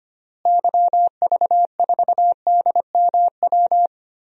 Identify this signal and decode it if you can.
Morse code, heard as a keyed tone, about 25 wpm, 700 Hz. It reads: YV4DMW
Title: DXCC-100 Call Signs - 6 Characters - Repeated 2x - 25wpm